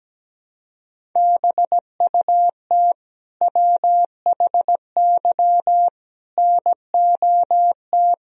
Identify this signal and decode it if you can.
Morse code, heard as a keyed tone, about 17 wpm, 700 Hz. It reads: BUT WHY NOT
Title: Sentences from Top 1000 Words - Review Part 1 - 17wpm